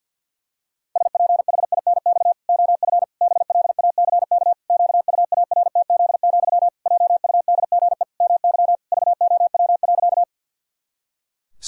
Morse code, heard as a keyed tone, about 50 wpm, 700 Hz. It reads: SPHINX OF BLACK QUARTZ, JUDGE MY VOW.
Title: Pangram Challenge v2 - 50wpm